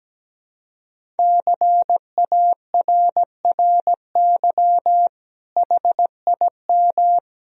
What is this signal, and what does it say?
Morse code, heard as a keyed tone, about 17 wpm, 700 Hz. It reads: CARRY HIM
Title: Sentences from Top 1000 Words - Review Part 1 - 17wpm